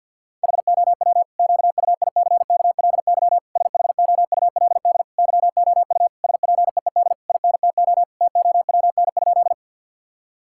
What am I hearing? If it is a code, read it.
Morse code, heard as a keyed tone, about 50 wpm, 700 Hz. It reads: HOW QUICKLY SHOULD YOU SPEED INTO TOWN?
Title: Sentences from Top 1000 Words - Review Part 5 - 50wpm